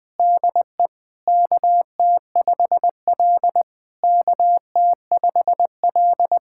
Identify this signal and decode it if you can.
Morse code, heard as a keyed tone, about 20 wpm, 700 Hz. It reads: DE KT5L KT5L
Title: SOTA QSO Practice 20wpm